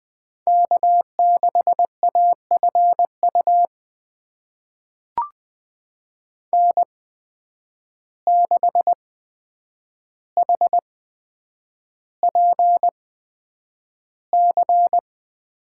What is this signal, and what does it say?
Morse code, heard as a keyed tone, about 20 wpm, spaced as for 9 wpm, 700 Hz. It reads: K6AFU N6HPC
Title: USA Call Signs - 8x Character Spacing - 20wpm